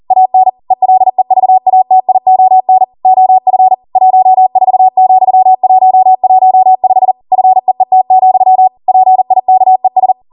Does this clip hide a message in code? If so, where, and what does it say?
Yes: Morse code from 0.1 to 10.1 s, heard as a keyed tone, about 40 wpm, 750 Hz. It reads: AN ELEVATION OF 14,115 FEET, PIKES